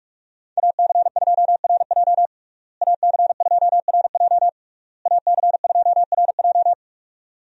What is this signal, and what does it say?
Morse code, heard as a keyed tone, about 41 wpm, 700 Hz. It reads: AC2RJ AC2RJ AC2RJ
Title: USA Call Signs - 5 Characters - Repeated 3x - 45wpm